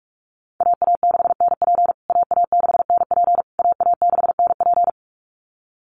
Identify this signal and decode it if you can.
Morse code, heard as a keyed tone, about 44 wpm, 700 Hz. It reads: AA6NP AA6NP AA6NP
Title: USA Call Signs Repeated 3x 45wpm